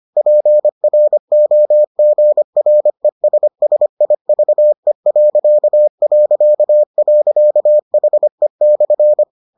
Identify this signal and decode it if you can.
Morse code, heard as a keyed tone, about 25 wpm, 600 Hz. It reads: PROGRESSIVE...HE/